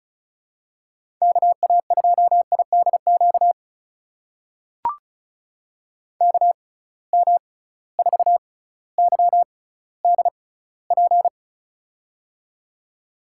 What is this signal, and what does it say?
Morse code, heard as a keyed tone, about 36 wpm, spaced as for 18 wpm, 700 Hz. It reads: KA2IDQ KM4YDP
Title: USA Call Signs - 6 Characters - 6x Character Spacing - 35wpm